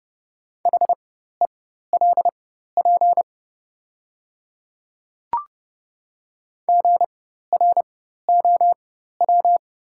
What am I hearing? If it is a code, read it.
Morse code, heard as a keyed tone, about 30 wpm, spaced as for 14 wpm, 700 Hz. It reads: HELP GROW
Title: 4x Character Spacing - Top 300 Words - 30wpm